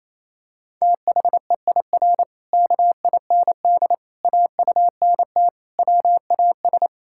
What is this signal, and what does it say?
Morse code, heard as a keyed tone, about 28 wpm, 700 Hz. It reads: THEIR KIND AUNT WAS
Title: Pangram Challenge v2 - 28wpm